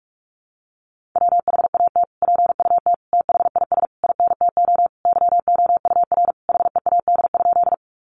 Morse code, heard as a keyed tone, about 44 wpm, 700 Hz. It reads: WHAT PUT THIS INTO YOUR HEAD?
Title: Sentences from Top 600 Words 45wpm